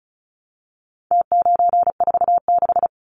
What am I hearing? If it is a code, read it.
Morse code, heard as a keyed tone, about 34 wpm, 700 Hz. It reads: T946
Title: Numbers - Contest Style - 4-Digit 35wpm